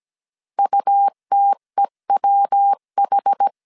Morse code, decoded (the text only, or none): UTEWH